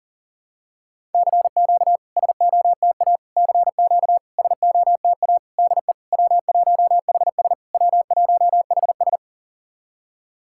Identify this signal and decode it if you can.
Morse code, heard as a keyed tone, about 40 wpm, 700 Hz. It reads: CQ SOTA CQ SOTA DE W1HS W1HS
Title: SOTA QSO Practice 40wpm